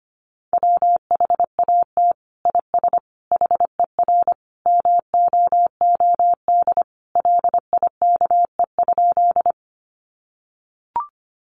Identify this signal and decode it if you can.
Morse code, heard as a keyed tone, about 25 wpm, 700 Hz. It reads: WHAT IS HER MOOD LIKE?
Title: Sentences from Top 1100 Words 25wpm